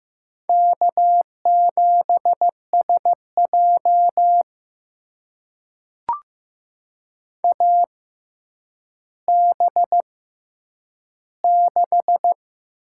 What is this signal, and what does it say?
Morse code, heard as a keyed tone, about 15 wpm, spaced as for 9 wpm, 700 Hz. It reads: K7SJ AB6
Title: USA Call Signs - 4 Characters - 6x Character Spacing - 15wpm